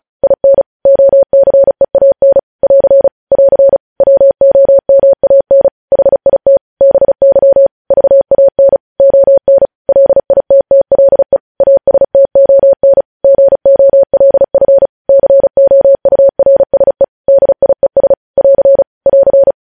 Morse code, heard as a keyed tone, about 36 wpm, 550 Hz. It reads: IN OCEAN <AR> <AR> WOMAN HIT BY VAN ON LITTLE ASTON GOLF COURSE DIES <AR> <AR>
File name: podcast / News headlines-1x-35x35